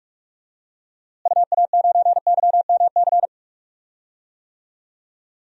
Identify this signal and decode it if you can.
Morse code, heard as a keyed tone, about 43 wpm, 700 Hz. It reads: UA9YMC
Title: DXCC-100 Call Signs - 6 Characters - Repeated 2x - 45wpm